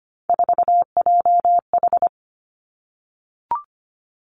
Morse code, heard as a keyed tone, about 25 wpm, 700 Hz. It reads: 4JH
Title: Three Challenging Letter-Number Combinations 25wpm